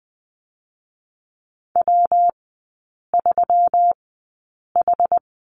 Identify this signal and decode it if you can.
Morse code, heard as a keyed tone, about 20 wpm, spaced as for 8 wpm, 700 Hz. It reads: W3H
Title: 152 Lesson 25e - Letter p - Three Characters 20wpm